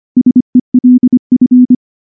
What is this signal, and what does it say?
Morse code, heard as a keyed tone, about 24 wpm, 250 Hz. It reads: SELF
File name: morse-training-25WPM